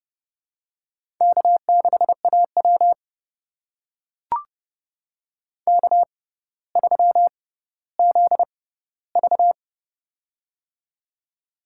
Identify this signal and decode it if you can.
Morse code, heard as a keyed tone, about 30 wpm, spaced as for 15 wpm, 700 Hz. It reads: K6AW K3ZV
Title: USA Call Signs - 4 Characters - 6x Character Spacing - 30wpm